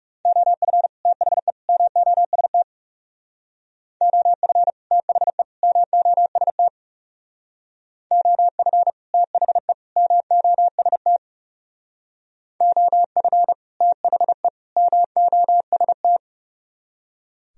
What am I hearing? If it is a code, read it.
Morse code, heard as a keyed tone, about 40 wpm, 700 Hz. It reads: OF THE MOST OF THE MOST OF THE MOST OF THE MOST